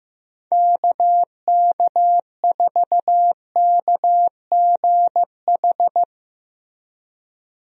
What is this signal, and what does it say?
Morse code, heard as a keyed tone, about 15 wpm, 700 Hz. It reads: KK4KGH
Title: USA Call Signs - 6 Characters - Repeated 2x - 15wpm